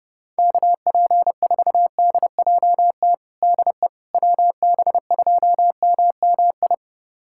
Morse code, heard as a keyed tone, about 32 wpm, 690 Hz. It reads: KP4DJT DE WB2MMI
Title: USA Callsign DE Callsign - 6 Characters - 30wpm